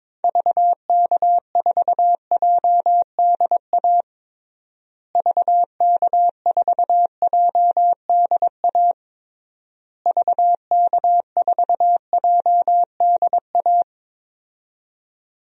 Morse code, heard as a keyed tone, about 22 wpm, 700 Hz. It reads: VK4JDA VK4JDA VK4JDA
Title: DXCC-100 Call Signs - 6 Characters - Repeated 3x - 22wpm